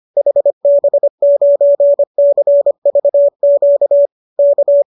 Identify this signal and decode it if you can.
Morse code, heard as a keyed tone, about 25 wpm, 570 Hz. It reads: HB9CVQ K
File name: qso210 25